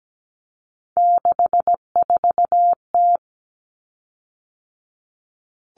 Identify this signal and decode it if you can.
Morse code, heard as a keyed tone, about 17 wpm, 700 Hz. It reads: 64T
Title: Common Cut Numbers - 3-Digit 17wpm